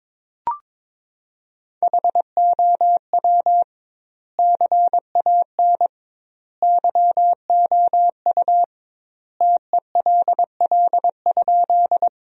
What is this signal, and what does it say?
Morse code, heard as a keyed tone, about 22 wpm, 700 Hz. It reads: HOW CAN YOU TELL?
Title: 2x Word-Spacing - Sentences from Top 200 Words - 22wpm